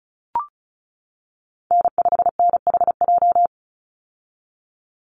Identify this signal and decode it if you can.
Morse code, heard as a keyed tone, about 34 wpm, 700 Hz. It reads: N5NHJ
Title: Call Signs - 35wpm